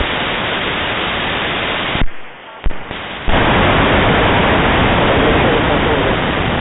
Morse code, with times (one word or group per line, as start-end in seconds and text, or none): none